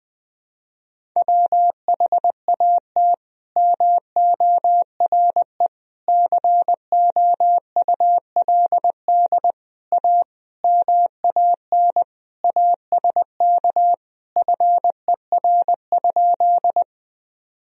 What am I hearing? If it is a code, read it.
Morse code, heard as a keyed tone, about 20 wpm, 700 Hz. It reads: WHAT MORE COULD A MAN ASK FER?
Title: Sentences from Top 200 Words and Abbreviations 20wpm